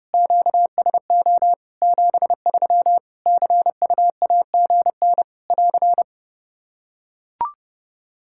Morse code, heard as a keyed tone, about 30 wpm, 700 Hz. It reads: QSO 73 CUAGN <AR>
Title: Practice QSO - Exchange 4 - Ending - 30wpm